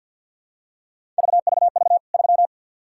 Morse code, heard as a keyed tone, about 52 wpm, 700 Hz. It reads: VVV 3
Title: K1USN SST QSO Practice 50wpm